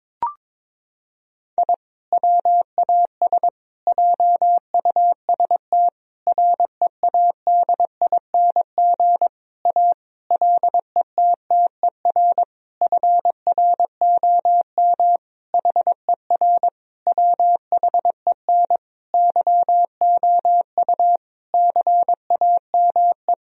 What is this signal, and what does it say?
Morse code, heard as a keyed tone, about 22 wpm, 690 Hz. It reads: I WAS JUST READING A LETTER FROM HER WHEN YOU CAME
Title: Sentences from Top 1000 Words - Review Part 3 - 22wpm